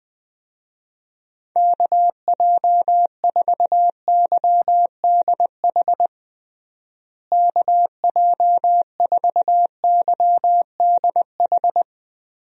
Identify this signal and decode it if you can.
Morse code, heard as a keyed tone, about 21 wpm, 700 Hz. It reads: KJ4YDH KJ4YDH
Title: USA Call Signs - 6 Characters - Repeated 2x - 20wpm